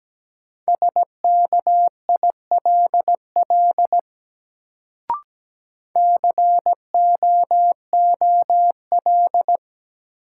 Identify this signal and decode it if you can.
Morse code, heard as a keyed tone, about 17 wpm, 700 Hz. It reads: SKILL COOL
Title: Sentences from Top 1200 Words - New Word Drill - 17wpm